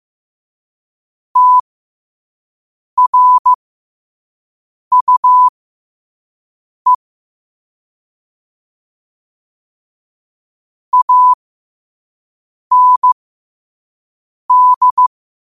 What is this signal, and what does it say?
Morse code, heard as a keyed tone, about 15 wpm, spaced as for 5 wpm, 1000 Hz. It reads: TRUE AND
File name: qotdfiles / 5wpm